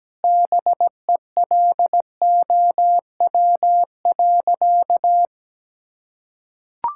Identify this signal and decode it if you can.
Morse code, heard as a keyed tone, about 17 wpm, 700 Hz. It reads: BELOW.
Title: Sentences from Top 1000 Words - Review Part 5 - 17wpm